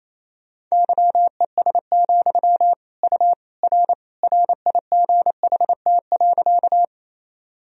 Morse code, heard as a keyed tone, about 28 wpm, 700 Hz. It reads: YES, U R RIGHT.
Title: Sentences from Top 500 Words and Abbreviations 28wpm